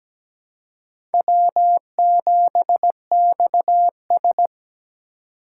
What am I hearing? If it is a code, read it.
Morse code, heard as a keyed tone, about 17 wpm, 700 Hz. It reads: W7XS
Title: USA Call Signs - 4 Characters - 2x Character Spacing - 17wpm